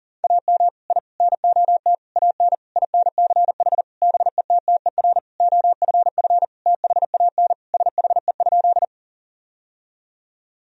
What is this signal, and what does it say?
Morse code, heard as a keyed tone, about 40 wpm, 700 Hz. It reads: AM I NOT AN INCH BETTER OFF THAN SHE?
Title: Sentences from Top 1000 Words - Review Part 3 - 40wpm